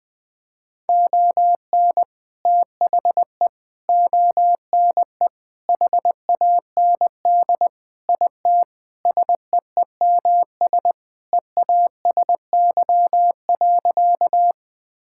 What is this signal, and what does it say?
Morse code, heard as a keyed tone, about 20 wpm, 700 Hz. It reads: ON THE ONE HAND IT SEEMS EASY.
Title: Sentences with 4-Word Phrases - 20wpm